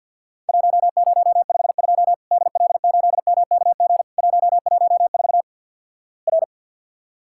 Morse code, heard as a keyed tone, about 48 wpm, 700 Hz. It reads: 1052 DL8MKG 114 R
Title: Worked All Europe DX Contest - QTC Exchange with Repeats - 50wpm